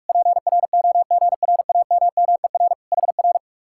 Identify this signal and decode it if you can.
Morse code, heard as a keyed tone, about 65 wpm, spaced as for 55 wpm, 700 Hz. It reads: PROGRAMMER HR
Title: Speed-Racing Common QSO Phrases - 45wpm